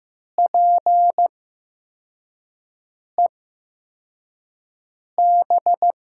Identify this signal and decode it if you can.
Morse code, heard as a keyed tone, about 15 wpm, spaced as for 4 wpm, 700 Hz. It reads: PEB
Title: USA Call Signs - 8x Character Spacing - 15wpm